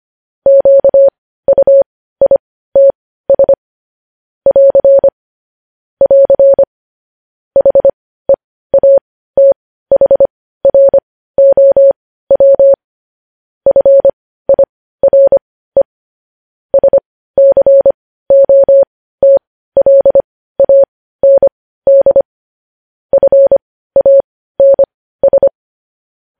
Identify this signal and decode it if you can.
Morse code, heard as a keyed tone, about 25 wpm, spaced as for 15 wpm, 550 Hz. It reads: QUITS <AR> <AR> HEATHROW FIRE SCOTLAND FANS
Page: news-headlines-in-morse-code-at-15-wpm